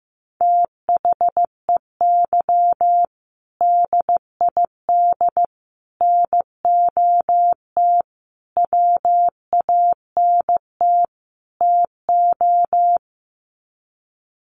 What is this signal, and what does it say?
Morse code, heard as a keyed tone, about 15 wpm, 700 Hz. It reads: THEY DID NOT WANT TO
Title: Sets of 5 Words 15wpm